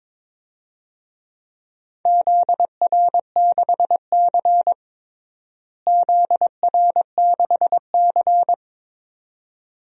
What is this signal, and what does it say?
Morse code, heard as a keyed tone, about 22 wpm, 700 Hz. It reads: ZR6C ZR6C
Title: DXCC-100 Call Signs - 4 Characters - Repeated 2x - 22wpm